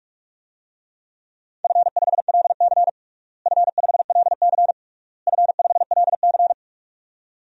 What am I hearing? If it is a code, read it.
Morse code, heard as a keyed tone, about 44 wpm, 700 Hz. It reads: F5LC F5LC F5LC
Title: DXCC-100 Call Signs - 4 Characters - Repeated 3x - 45wpm